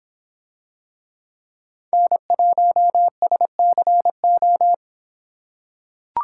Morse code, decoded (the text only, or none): N1SCO